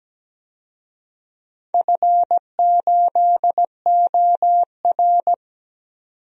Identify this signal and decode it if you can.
Morse code, heard as a keyed tone, about 17 wpm, 700 Hz. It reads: F8OR
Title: DXCC-100 Call Signs - 4 Characters - 17wpm